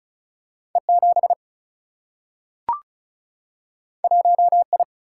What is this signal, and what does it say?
Morse code, heard as a keyed tone, about 36 wpm, 700 Hz. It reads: E7 1I
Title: Single-Letter Single-Number 35wpm